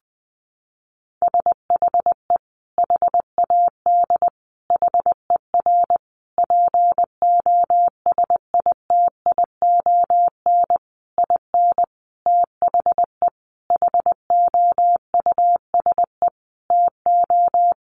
Morse code, heard as a keyed tone, about 20 wpm, 700 Hz. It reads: SHE HAD HER POSITION IN THE HOUSE TO
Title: Sentences from Top 900 Words 20wpm